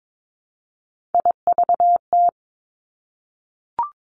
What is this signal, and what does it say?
Morse code, heard as a keyed tone, about 22 wpm, 700 Hz. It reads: IVT
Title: Three Challenging Letter-Number Combinations 22wpm